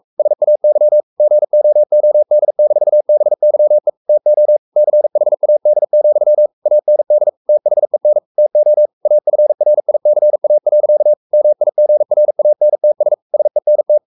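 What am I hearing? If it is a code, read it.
Morse code, heard as a keyed tone, about 44 wpm, 600 Hz. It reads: SAY GOOD-BYE TO CHAD, AND THEN TO AFRICA. MIGRANTS SENT
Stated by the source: Morse code at 43 wpm sending Ken Follett's novel Never in English, from CW med Gnister.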